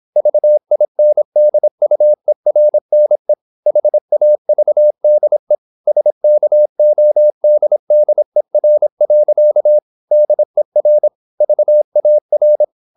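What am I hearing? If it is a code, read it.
Morse code, heard as a keyed tone, about 26 wpm, 600 Hz. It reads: VINDUERNE HAVDE SKODDER. DER VAR
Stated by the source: High-speed (QRQ) Morse at 26 wpm.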